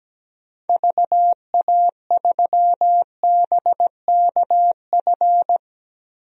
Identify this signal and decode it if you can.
Morse code, heard as a keyed tone, about 17 wpm, 700 Hz. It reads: VA3BKF
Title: DXCC-100 Call Signs - 6 Characters - Repeated 3x - 17wpm